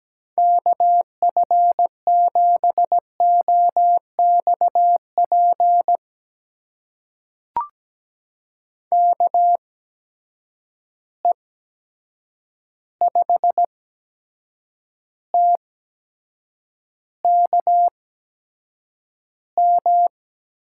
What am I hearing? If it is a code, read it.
Morse code, heard as a keyed tone, about 17 wpm, spaced as for 7 wpm, 700 Hz. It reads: KF7OXP KE5TKM
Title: USA Call Signs - 6 Characters - 8x Character Spacing - 17wpm